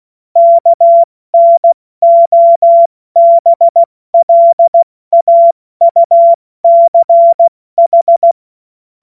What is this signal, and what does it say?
Morse code, heard as a keyed tone, about 16 wpm, 680 Hz. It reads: KNOBLAUCH
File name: Morsekurs / Witze / Witz 04. Mai 2021, 70 F80